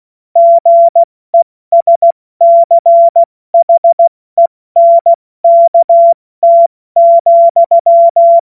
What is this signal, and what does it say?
Morse code, heard as a keyed tone, about 16 wpm, 680 Hz. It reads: GESCHENKT,
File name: Morsekurs / Witze / Witz 28. März 2023 70 F80